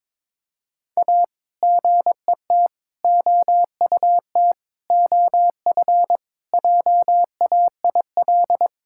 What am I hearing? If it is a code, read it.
Morse code, heard as a keyed tone, about 21 wpm, 700 Hz. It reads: A GET OUT OF JAIL
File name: practice / 22wpm / Sets of 7 Words - 22wpm